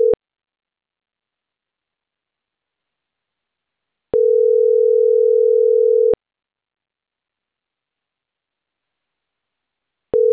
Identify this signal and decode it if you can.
Ringback tone (North America), 440 + 480 Hz.